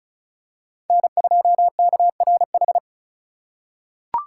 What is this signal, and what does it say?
Morse code, heard as a keyed tone, about 34 wpm, 700 Hz. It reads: N2KRH
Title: USA Call Signs - 5 Characters - 4x Character Spacing - 35wpm